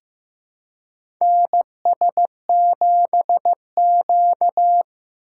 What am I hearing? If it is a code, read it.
Morse code, heard as a keyed tone, about 15 wpm, 700 Hz. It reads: NS7Q
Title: USA Call Signs - 4 Characters - 2x Character Spacing - 15wpm